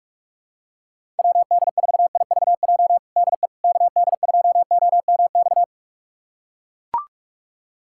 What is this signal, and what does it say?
Morse code, heard as a keyed tone, about 41 wpm, 700 Hz. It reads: WD4IVJ DE KD2OMX
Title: USA Callsign DE Callsign 45wpm